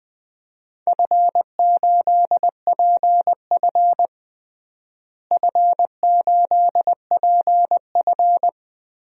Morse code, heard as a keyed tone, about 20 wpm, 700 Hz. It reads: F8PF F8PF
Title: DXCC-100 Call Signs - 4 Characters - Repeated 2x - 20wpm